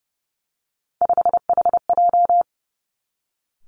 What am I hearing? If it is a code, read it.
Morse code, heard as a keyed tone, about 30 wpm, 700 Hz. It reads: 5HJ
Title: Three Challenging Letter-Number Combinations 30wpm